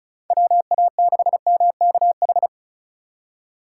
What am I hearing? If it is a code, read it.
Morse code, heard as a keyed tone, about 36 wpm, 700 Hz. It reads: WA6MKH
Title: USA Call Signs - 6 Characters - 4x Character Spacing - 35wpm